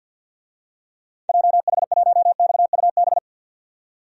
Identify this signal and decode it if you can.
Morse code, heard as a keyed tone, about 51 wpm, 700 Hz. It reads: JH1XUB
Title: DXCC-100 Call Signs - 6 Characters - 50wpm